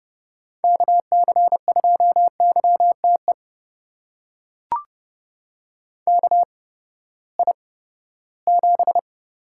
Morse code, heard as a keyed tone, about 30 wpm, spaced as for 17 wpm, 700 Hz. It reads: KC2YTE KI7
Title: USA Call Signs - 6 Characters - 8x Character Spacing - 30wpm